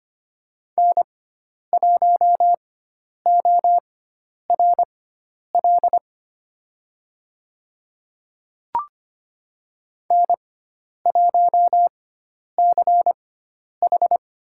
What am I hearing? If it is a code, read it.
Morse code, heard as a keyed tone, about 25 wpm, spaced as for 10 wpm, 700 Hz. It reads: N1ORL N1CH
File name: Lesson 16g - Letter c - Call Signs 25wpm